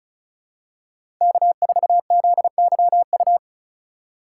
Morse code, heard as a keyed tone, about 34 wpm, 700 Hz. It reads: K4ZYU
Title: USA Call Signs - 5 Characters - 2x Character Spacing - 35wpm